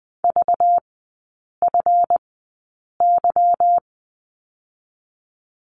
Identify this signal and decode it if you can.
Morse code, heard as a keyed tone, about 20 wpm, spaced as for 8 wpm, 700 Hz. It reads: VFY
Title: 191 Lesson 31e - Letter v - Three Characters 20wpm